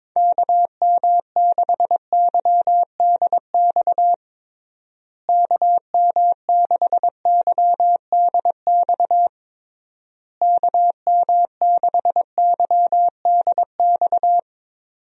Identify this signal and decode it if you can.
Morse code, heard as a keyed tone, about 22 wpm, 700 Hz. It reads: KM6YDX KM6YDX KM6YDX
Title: USA Call Signs - 6 Characters - Repeated 3x - 22wpm